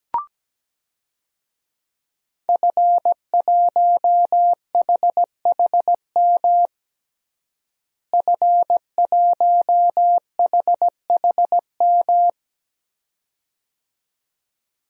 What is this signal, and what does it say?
Morse code, heard as a keyed tone, about 17 wpm, 700 Hz. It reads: F1HHM F1HHM
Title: DXCC-100 Call Signs - 5 Characters - Repeated 2x - 17wpm